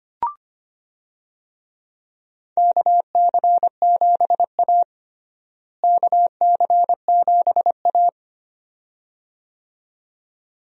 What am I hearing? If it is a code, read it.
Morse code, heard as a keyed tone, about 25 wpm, 700 Hz. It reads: KC7A KC7A
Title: USA Call Signs - 4 Characters - Repeated 2x - 25wpm